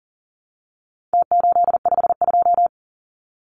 Morse code, heard as a keyed tone, about 40 wpm, 700 Hz. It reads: T852
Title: Numbers - Contest Style - 4-Digit 40wpm